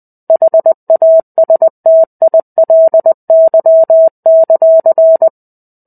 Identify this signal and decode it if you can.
Morse code, heard as a keyed tone, about 20 wpm, 650 Hz. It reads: HASTILY;
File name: 05-Chapter-05 20 WPM